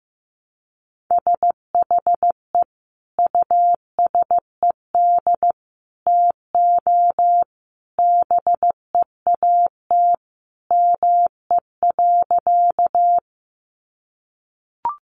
Morse code, heard as a keyed tone, about 15 wpm, 700 Hz. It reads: SHE USED TO BEAT ME.
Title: Sentences from Top 1300 Words 15wpm